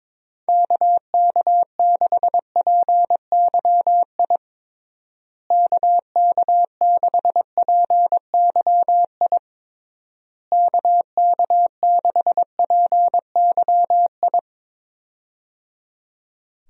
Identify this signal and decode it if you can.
Morse code, heard as a keyed tone, about 22 wpm, 700 Hz. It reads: KK6PYI KK6PYI KK6PYI
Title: USA Call Signs - 6 Characters - Repeated 3x - 22wpm